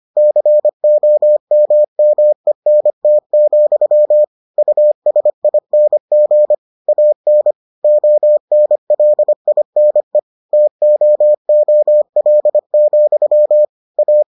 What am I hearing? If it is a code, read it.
Morse code, heard as a keyed tone, about 25 wpm, 600 Hz. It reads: COMMENT, USING AN ONLINE TOOL, A